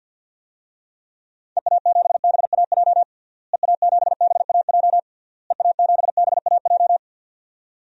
Morse code, heard as a keyed tone, about 45 wpm, 700 Hz. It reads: EA7BAJ EA7BAJ EA7BAJ
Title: DXCC-100 Call Signs - 6 Characters - Repeated 3x - 50wpm